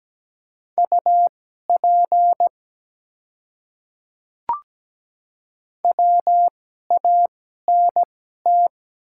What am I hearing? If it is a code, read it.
Morse code, heard as a keyed tone, about 17 wpm, spaced as for 12 wpm, 700 Hz. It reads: UP WANT
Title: 2x Character Spacing - Top 200 Words - 17wpm